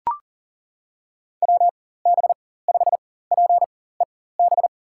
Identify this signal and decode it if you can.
Morse code, heard as a keyed tone, about 37 wpm, spaced as for 18 wpm, 700 Hz. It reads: WB5PEB
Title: USA Call Signs - 6 Characters - 4x Character Spacing - 40wpm